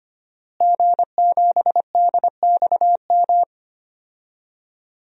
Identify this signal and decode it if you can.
Morse code, heard as a keyed tone, about 25 wpm, 700 Hz. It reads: G7DXM
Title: DXCC-100 Call Signs - 5 Characters - Repeated 2x - 25wpm